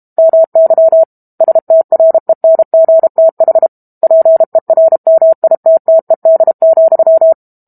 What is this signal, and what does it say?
Morse code, heard as a keyed tone, about 32 wpm, 650 Hz. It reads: MY STRENGTH PERMITTED,
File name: Chapter 6 - 32 WPM